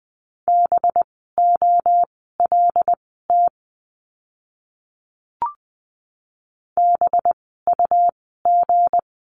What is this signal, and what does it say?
Morse code, heard as a keyed tone, about 20 wpm, spaced as for 15 wpm, 700 Hz. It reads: BOLT BUG
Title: 185 Lesson 30f - Letter b - Words 20wpm